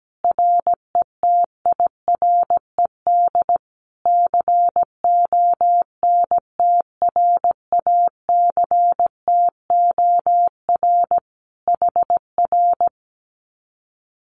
Morse code, heard as a keyed tone, about 17 wpm, 700 Hz. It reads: RETIRED CONTRACTOR HR
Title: Common QSO Phrases Encore 17wpm